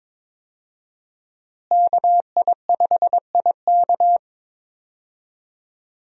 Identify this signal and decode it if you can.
Morse code, heard as a keyed tone, about 22 wpm, 700 Hz. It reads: KI5IK
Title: USA Call Signs - 5 Characters - Repeated 2x - 22wpm